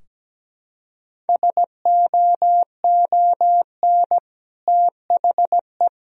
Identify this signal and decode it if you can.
Morse code, heard as a keyed tone, about 17 wpm, 700 Hz. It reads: SOON THE
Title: Sentences from Top 1000 Words - Review Part 3 - 17wpm